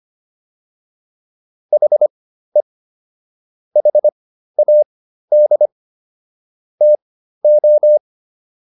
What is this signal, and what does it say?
Morse code, heard as a keyed tone, about 25 wpm, spaced as for 13 wpm, 600 Hz. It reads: HE HAD TO